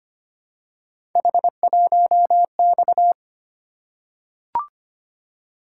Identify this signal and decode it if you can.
Morse code, heard as a keyed tone, about 25 wpm, 700 Hz. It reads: H1X
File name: Three Letter-Number Combinations - 25wpm